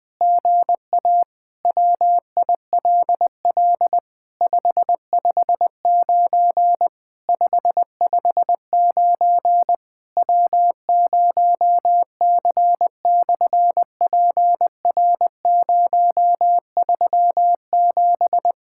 Morse code, heard as a keyed tone, about 20 wpm, 700 Hz. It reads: GA WILL 559 559 W0C/PR037
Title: SOTA QSO Practice 20wpm